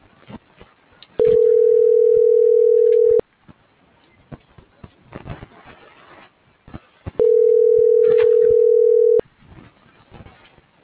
Ringback tone (North America), 440 + 480 Hz.